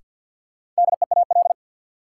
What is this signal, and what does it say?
Morse code, heard as a keyed tone, about 53 wpm, 700 Hz. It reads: DEAL
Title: Sentences from Top 800 Words - New Word Drill - 50wpm